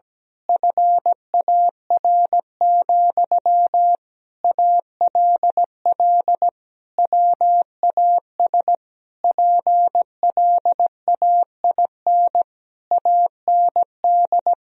Morse code, heard as a keyed tone, about 17 wpm, 700 Hz. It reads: FAR, ALL WAS PLAIN AND
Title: Sentences from Top 1000 Words - Review Part 4 - 17wpm